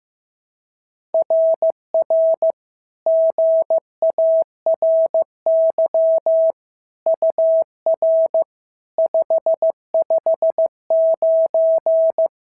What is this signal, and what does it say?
Morse code, heard as a keyed tone, about 14 wpm, 650 Hz. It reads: RR GARY UR 559
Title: SOTA QSO Practice 15wpm